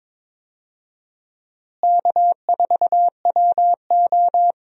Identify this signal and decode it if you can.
Morse code, heard as a keyed tone, about 21 wpm, 700 Hz. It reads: K4WO
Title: USA Call Signs - 4 Characters - Repeated 2x - 22wpm